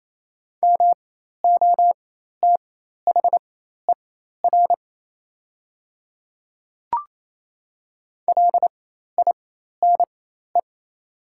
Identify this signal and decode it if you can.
Morse code, heard as a keyed tone, about 28 wpm, spaced as for 13 wpm, 700 Hz. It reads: MOTHER LINE
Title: 4x Character Spacing - Top 200 Words - 28wpm